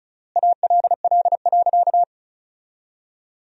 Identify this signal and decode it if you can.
Morse code, heard as a keyed tone, about 34 wpm, 700 Hz. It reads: ALL.
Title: 3x Word-Spacing - Sentences from Top 500 Words - 35wpm